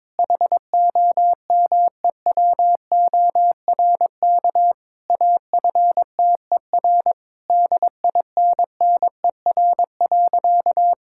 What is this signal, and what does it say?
Morse code, heard as a keyed tone, about 22 wpm, 700 Hz. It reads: HOMEWORK AFTER DINNER.
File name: Difficult Sentences with Hints - 22wpm